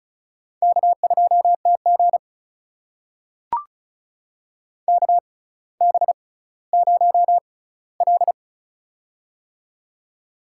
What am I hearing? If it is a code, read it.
Morse code, heard as a keyed tone, about 36 wpm, spaced as for 18 wpm, 700 Hz. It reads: K2TG KB0L
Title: USA Call Signs - 4 Characters - 6x Character Spacing - 35wpm